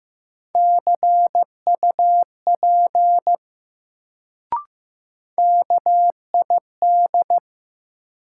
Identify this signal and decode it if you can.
Morse code, heard as a keyed tone, about 15 wpm, 700 Hz. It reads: CUP KID
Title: Sentences from Top 1600 Words - New Word Drill Alt - 15wpm